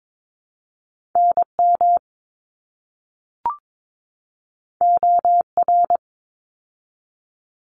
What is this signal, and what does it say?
Morse code, heard as a keyed tone, about 22 wpm, 700 Hz. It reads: NM OR
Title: ARRL Sections 22wpm